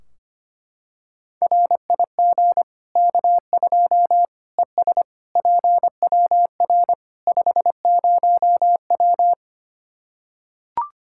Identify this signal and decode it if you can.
Morse code, heard as a keyed tone, about 25 wpm, 700 Hz. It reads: RIG K2 ES PWR 50W
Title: Practice QSO - Exchange 2 - Info - 25wpm